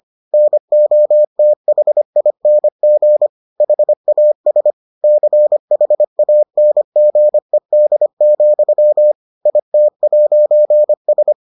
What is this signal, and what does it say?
Morse code, heard as a keyed tone, about 25 wpm, 600 Hz. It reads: NOTHING HAS CHANGED, IT'S